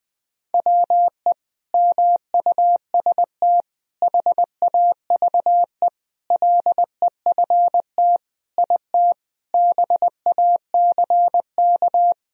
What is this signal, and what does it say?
Morse code, heard as a keyed tone, about 20 wpm, 700 Hz. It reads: WE MUST HAVE LEFT IT BACK
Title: Sentences from Top 1000 Words - Review Part 2 - 20wpm